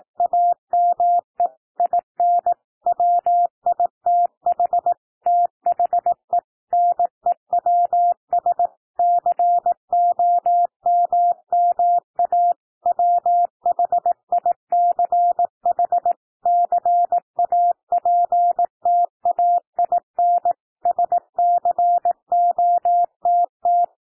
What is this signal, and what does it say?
Morse code, heard as a keyed tone, about 18 wpm, 700 Hz. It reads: AME IN WITH THE NEWS COMMA WHICH CAPTAIN SCOTT
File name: TEXT PASSAGE 18 wpm